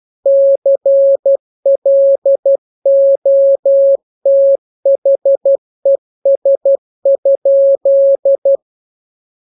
Morse code, heard as a keyed tone, about 12 wpm, 550 Hz. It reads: CLOTHES?
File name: jokes / joke 19 12 0000